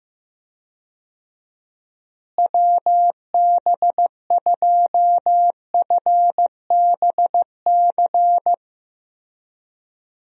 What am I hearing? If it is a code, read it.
Morse code, heard as a keyed tone, about 15 wpm, 700 Hz. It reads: WB2FBC
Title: USA Call Signs - 6 Characters - Repeated 3x - 15wpm